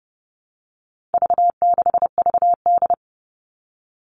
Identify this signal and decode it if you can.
Morse code, heard as a keyed tone, about 30 wpm, 700 Hz. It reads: V6VD
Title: All Cut Numbers - 4-Digit 30wpm